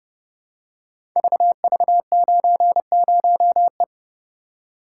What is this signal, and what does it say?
Morse code, heard as a keyed tone, about 30 wpm, 700 Hz. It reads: VV90E
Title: Code Groups - 5 Letter:Numbers - 30wpm